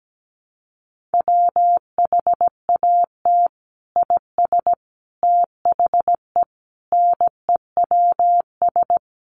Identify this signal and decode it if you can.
Morse code, heard as a keyed tone, about 17 wpm, 700 Hz. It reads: WHAT IS THE NEWS
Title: Sentences from Top 800 Words 17wpm